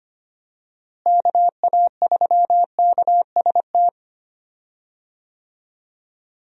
Morse code, heard as a keyed tone, about 25 wpm, 700 Hz. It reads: KA3KST
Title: USA Call Signs - 6 Characters - Repeated 3x - 25wpm